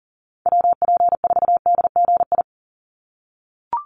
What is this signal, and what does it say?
Morse code, heard as a keyed tone, about 40 wpm, 700 Hz. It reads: WP4DGI
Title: USA Call Signs 40wpm